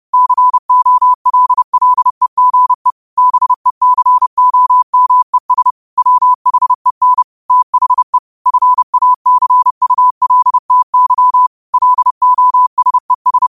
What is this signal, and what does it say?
Morse code, heard as a keyed tone, about 30 wpm, 1000 Hz. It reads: COLLEGE BECOMES WHEN THE FACULTY LOSES
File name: qotdfiles / 30wpm